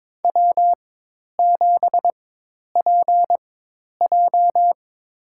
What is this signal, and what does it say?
Morse code, heard as a keyed tone, about 22 wpm, spaced as for 10 wpm, 700 Hz. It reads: W7PJ
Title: USA Call Signs - 4 Characters - 4x Character Spacing - 22wpm